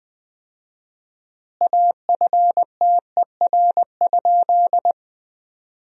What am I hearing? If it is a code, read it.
Morse code, heard as a keyed tone, about 20 wpm, 700 Hz. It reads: AFTER?
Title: 4x Word-Spacing - Sentences from Top 200 Words - 20wpm